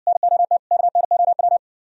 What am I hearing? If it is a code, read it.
Morse code, heard as a keyed tone, about 58 wpm, 700 Hz. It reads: NOT KNOW